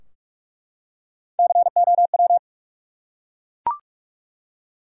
Morse code, heard as a keyed tone, about 44 wpm, 700 Hz. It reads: COW
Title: Three Letter Words - 45wpm